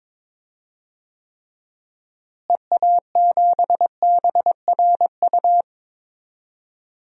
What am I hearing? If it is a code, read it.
Morse code, heard as a keyed tone, about 21 wpm, 700 Hz. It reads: EA7BRU
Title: DXCC-100 Call Signs - 6 Characters - Repeated 3x - 22wpm